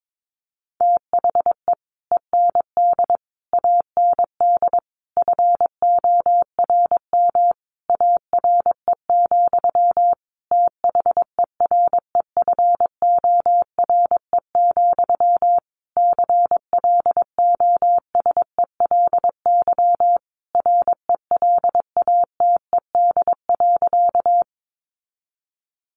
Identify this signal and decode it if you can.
Morse code, heard as a keyed tone, about 22 wpm, 700 Hz. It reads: THE END AND FORM ARE, THEREFORE, CLOSELY RELATED.
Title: Sentences from Top 1400 Words 22wpm